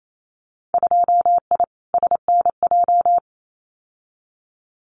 Morse code, heard as a keyed tone, about 28 wpm, 700 Hz. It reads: 2I SNJ
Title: Winter Field Day Exchanges Encore 28wpm